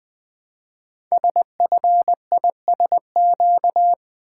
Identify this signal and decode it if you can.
Morse code, heard as a keyed tone, about 20 wpm, 700 Hz. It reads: SFISQ
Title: Code Groups - 5 Letter:Numbers - 20wpm